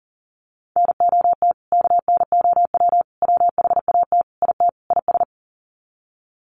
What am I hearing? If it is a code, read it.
Morse code, heard as a keyed tone, about 40 wpm, 700 Hz. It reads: NOT KNOW WHAT IT IS
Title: Sets of 5 Words Encore 40wpm